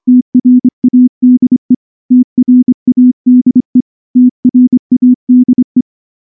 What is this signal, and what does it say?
Morse code, heard as a keyed tone, about 25 wpm, 250 Hz. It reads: TRADE TRADE TRADE
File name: morse-training-25WPM